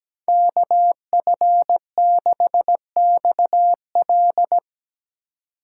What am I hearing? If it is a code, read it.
Morse code, heard as a keyed tone, about 19 wpm, 700 Hz. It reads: KF6XL
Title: Speed Racing - USA 2x2 Call Signs 17wpm